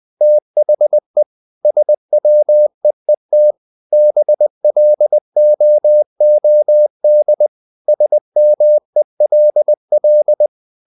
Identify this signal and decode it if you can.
Morse code, heard as a keyed tone, about 20 wpm, 600 Hz. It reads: THE SWEET BLOOD SMELL
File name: the old man and the sea / the old man and the sea part 1 20wpm